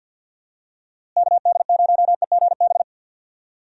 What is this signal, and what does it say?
Morse code, heard as a keyed tone, about 50 wpm, 700 Hz. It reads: KD0EGB
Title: USA Call Signs - 6 Characters - Repeated 2x - 50wpm